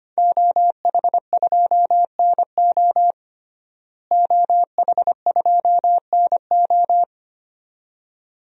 Morse code, heard as a keyed tone, about 25 wpm, 700 Hz. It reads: OH2NO OH2NO
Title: DXCC-100 Call Signs - 5 Characters - Repeated 2x - 25wpm